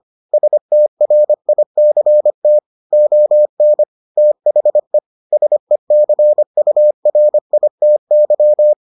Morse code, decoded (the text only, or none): STRICT ON THE SECURITY